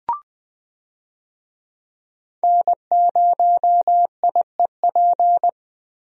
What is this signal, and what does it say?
Morse code, heard as a keyed tone, about 20 wpm, 700 Hz. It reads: N0IEP
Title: USA Call Signs - 5 Characters - Repeated 3x - 20wpm